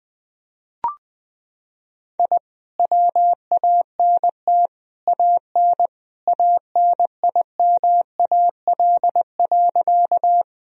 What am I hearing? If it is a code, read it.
Morse code, heard as a keyed tone, about 20 wpm, 700 Hz. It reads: I WANT AN ANIMAL.
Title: Sentences from Top 1000 Words - Review Part 1 - 20wpm